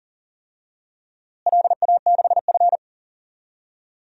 Morse code, heard as a keyed tone, about 40 wpm, 700 Hz. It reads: LA6F
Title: DXCC-100 Call Signs - 4 Characters - Repeated 3x - 40wpm